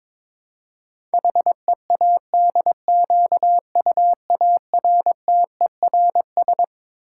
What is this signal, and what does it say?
Morse code, heard as a keyed tone, about 22 wpm, 700 Hz. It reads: HEADQUARTERS
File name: Twelve Letter Words - 22wpm